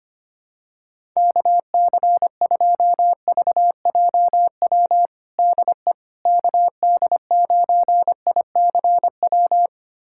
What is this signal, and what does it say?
Morse code, heard as a keyed tone, about 25 wpm, 700 Hz. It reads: KC2VJW DE KD9ICW
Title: USA Callsign DE Callsign 25wpm